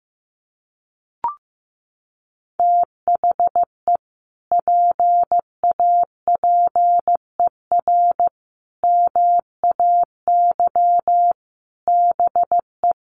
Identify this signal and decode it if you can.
Morse code, heard as a keyed tone, about 15 wpm, 700 Hz. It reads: THE PAPER MAY BE
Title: Sentences from Top 1400 Words 15wpm